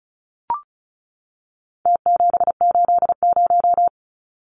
Morse code, heard as a keyed tone, about 34 wpm, 700 Hz. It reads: T780
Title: Numbers - Contest Style - 4-Digit 35wpm